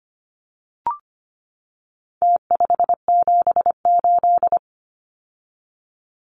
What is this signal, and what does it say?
Morse code, heard as a keyed tone, about 25 wpm, 700 Hz. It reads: T578
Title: Numbers - Contest Style - 4-Digit 25wpm